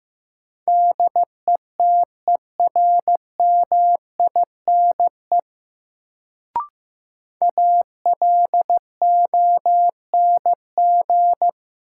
Morse code, heard as a keyed tone, about 15 wpm, 700 Hz. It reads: DETERMINE ALONG
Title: Sentences from Top 700 Words - New Word Drill - 15wpm